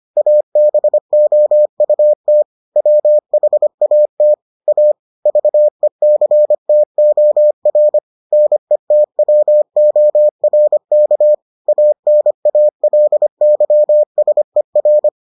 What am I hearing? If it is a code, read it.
Morse code, heard as a keyed tone, about 25 wpm, 600 Hz. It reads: ABOUT WHAT A VECTOR NETWORK ANALYSER